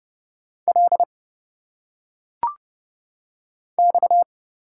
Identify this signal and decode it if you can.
Morse code, heard as a keyed tone, about 30 wpm, 700 Hz. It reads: L X
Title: Single Letters - Phonetic Pronunciation 30wpm